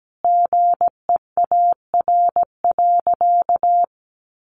Morse code, heard as a keyed tone, about 17 wpm, 700 Hz. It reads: GEAR.
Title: Sentences from Top 1400 Words 17wpm